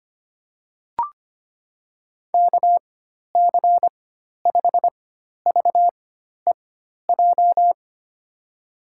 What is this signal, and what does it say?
Morse code, heard as a keyed tone, about 25 wpm, spaced as for 12 wpm, 700 Hz. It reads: KC5VEJ
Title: USA Call Signs - 4x Character Spacing - 25wpm